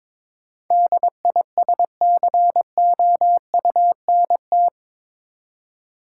Morse code, heard as a keyed tone, about 22 wpm, 700 Hz. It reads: DISCOUNT
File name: practice / 22wpm / Eight Letter Words - 22wpm